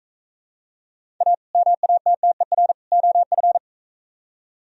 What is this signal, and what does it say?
Morse code, heard as a keyed tone, about 41 wpm, 700 Hz. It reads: A MATTER OF